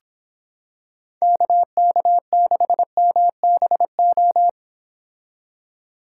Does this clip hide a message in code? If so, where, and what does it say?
Yes: Morse code from 1.2 to 4.5 s, heard as a keyed tone, about 26 wpm, 700 Hz. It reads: KK6MBO